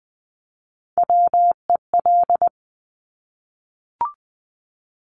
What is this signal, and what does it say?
Morse code, heard as a keyed tone, about 20 wpm, 700 Hz. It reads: WEL
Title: Top Three Letter Combinations in Words 20wpm